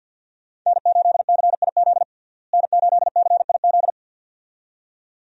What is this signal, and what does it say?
Morse code, heard as a keyed tone, about 50 wpm, 700 Hz. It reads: N8CIZ N8CIZ
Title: USA Call Signs - 5 Characters - Repeated 2x - 50wpm